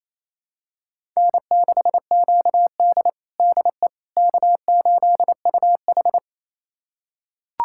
Morse code, heard as a keyed tone, about 28 wpm, 700 Hz. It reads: N6QD DE K8UH
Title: USA Callsign DE Callsign - 4 Characters - 28wpm